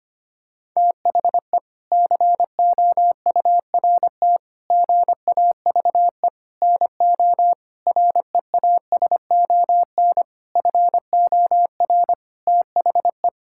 Morse code, heard as a keyed tone, about 25 wpm, 700 Hz. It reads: THE COURT GAVE NO REASON FOR THE
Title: Sentences from Top 1600 Words 25wpm